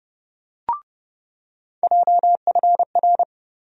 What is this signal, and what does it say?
Morse code, heard as a keyed tone, about 32 wpm, 700 Hz. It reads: JFR
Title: USA Call Sign Postfixes - 3 Letters 30wpm